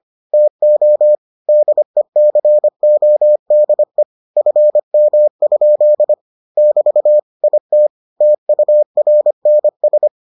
Morse code, heard as a keyed tone, about 25 wpm, 600 Hz. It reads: TO DECODE FM? <BT> IT TURNS